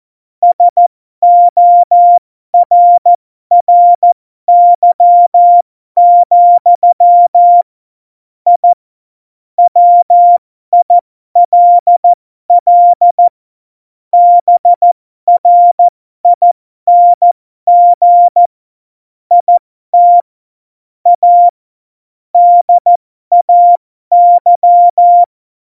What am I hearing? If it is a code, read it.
Morse code, heard as a keyed tone, about 14 wpm, spaced as for 12 wpm, 700 Hz. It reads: SORRY, I WILL BRING IT A DAY
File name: Morsekurs / Witze